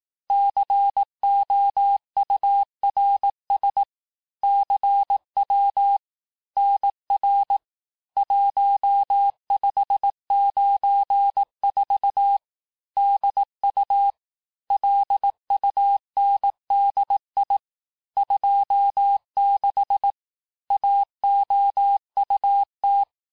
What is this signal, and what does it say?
Morse code, heard as a keyed tone, about 18 wpm, 800 Hz. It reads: COURS CW NR 1594 DU LUNDI 26 AOUT
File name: cours-du-lundi files / cours-1594--teste-chiffre-vitesse-17mpm